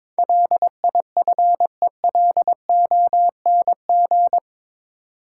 Morse code, heard as a keyed tone, about 22 wpm, 700 Hz. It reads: LIFELONG
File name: practice / 22wpm / Eight Letter Words - 22wpm